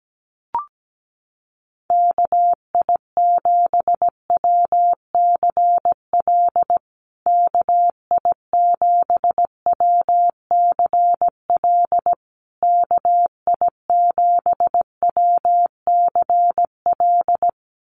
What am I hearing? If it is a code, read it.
Morse code, heard as a keyed tone, about 17 wpm, 700 Hz. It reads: KI7WCL KI7WCL KI7WCL
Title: USA Call Signs Repeated 3x 17wpm